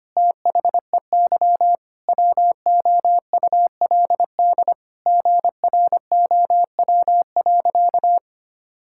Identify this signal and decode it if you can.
Morse code, heard as a keyed tone, about 25 wpm, 700 Hz. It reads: THEY WOULD GROW.
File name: practice / 25wpm / Sentences with 9-Word Phrases - 25wpm